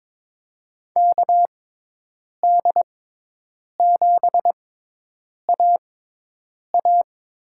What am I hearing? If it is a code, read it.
Morse code, heard as a keyed tone, about 22 wpm, spaced as for 8 wpm, 700 Hz. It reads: KD7AA
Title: USA Call Signs - 6x Character Spacing - 22wpm